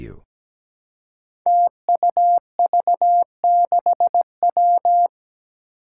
Morse code, heard as a keyed tone, about 17 wpm, 700 Hz. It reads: TUV6W
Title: Five Challenging Letter-Number Combinations 17wpm